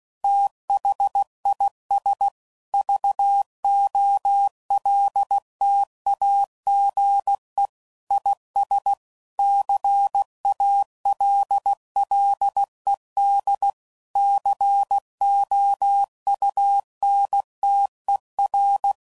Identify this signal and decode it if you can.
Morse code, heard as a keyed tone, about 16 wpm, 800 Hz. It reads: THIS VOLTAGE IS CALLED COUNTER
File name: CW Code Groups @15WPM/Technical Subjects